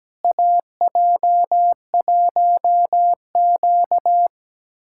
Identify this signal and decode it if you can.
Morse code, heard as a keyed tone, about 17 wpm, 700 Hz. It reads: AJ1Q
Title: USA Callsign DE Callsign - 4 Characters - 17wpm